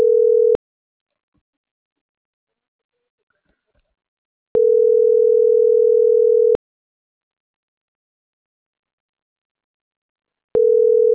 Ringback tone (North America), 440 + 480 Hz.